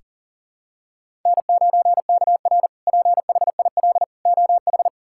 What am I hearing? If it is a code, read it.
Morse code, heard as a keyed tone, about 37 wpm, 700 Hz. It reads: N9KR PHIL OH
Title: K1USN SST QSO Practice - 40wpm